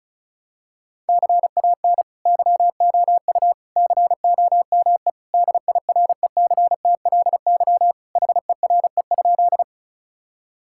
Morse code, heard as a keyed tone, about 36 wpm, 700 Hz. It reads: CAN YOU COME DIRECTLY HERE?
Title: Sentences from Top 1000 Words - Review Part 5 - 35wpm